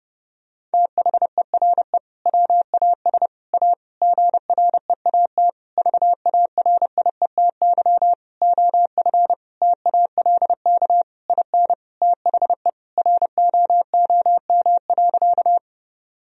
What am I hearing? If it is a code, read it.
Morse code, heard as a keyed tone, about 30 wpm, 700 Hz. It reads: THERE WAS A GREAT VARIETY OF TALK IN THE ROOM.
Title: Sentences from Top 1000 Words - Review Part 4 - 30wpm